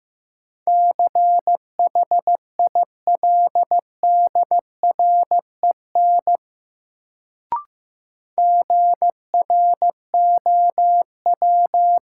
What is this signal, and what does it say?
Morse code, heard as a keyed tone, about 15 wpm, 700 Hz. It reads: CHILDREN GROW
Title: Sentences from Top 300 Words - New Word Drill - 15wpm